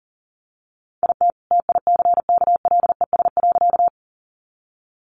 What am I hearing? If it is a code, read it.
Morse code, heard as a keyed tone, about 40 wpm, 700 Hz. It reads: IT TICKLES.
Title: Additional Easy Phrases 40wpm